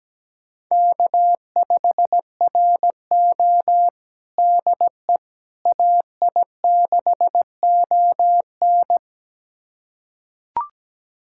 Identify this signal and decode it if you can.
Morse code, heard as a keyed tone, about 17 wpm, 700 Hz. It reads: K5RO DE AI6ON
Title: USA Callsign DE Callsign 17wpm